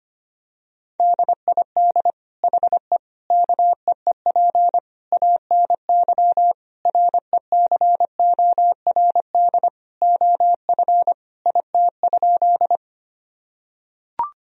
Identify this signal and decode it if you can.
Morse code, heard as a keyed tone, about 25 wpm, 700 Hz. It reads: DID HE KEEP ANY RECORD OF IT?
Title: Sentences from Top 1000 Words - Review Part 3 - 25wpm